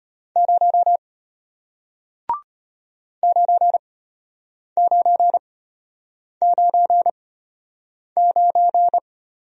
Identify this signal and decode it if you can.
Morse code, heard as a keyed tone, about 34 wpm, spaced as for 8 wpm, 700 Hz. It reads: 0 9999